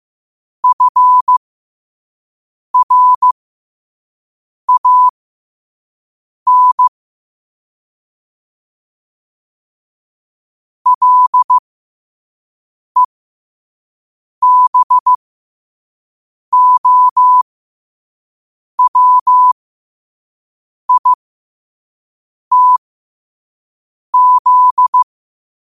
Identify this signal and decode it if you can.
Morse code, heard as a keyed tone, about 15 wpm, spaced as for 5 wpm, 1000 Hz. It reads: FRAN LEBOWITZ
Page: New quotes every day in morse code at 5 Words per minute.